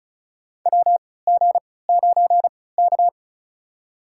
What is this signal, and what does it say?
Morse code, heard as a keyed tone, about 36 wpm, spaced as for 20 wpm, 700 Hz. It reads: WG9K
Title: USA Call Signs - 4 Characters - 3x Character Spacing - 35wpm